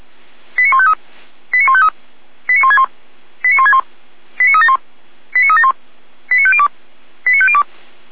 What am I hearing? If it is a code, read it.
Five-tone selective call (ZVEI), heard as quick runs of five tones, tones of about 70 ms. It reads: 89153 89153 89161 89161 89371 89371 89502 89502
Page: audio.f85360.feuerwehr-orlamuende.org